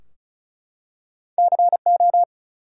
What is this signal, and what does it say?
Morse code, heard as a keyed tone, about 33 wpm, 700 Hz. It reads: CO
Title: US States and Canadian Provinces 35wpm